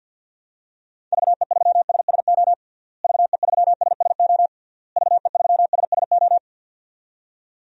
Morse code, heard as a keyed tone, about 49 wpm, 700 Hz. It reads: VE3SSO VE3SSO VE3SSO
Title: DXCC-100 Call Signs - 6 Characters - Repeated 3x - 50wpm